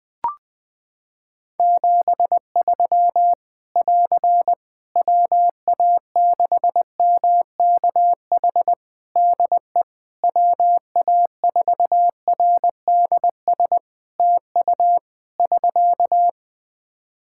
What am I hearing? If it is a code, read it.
Morse code, heard as a keyed tone, about 20 wpm, 700 Hz. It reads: 73 <AR> WA6MKH DE WA4RDS TU <SK>
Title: Practice QSO - Exchange 4 - Ending - 20wpm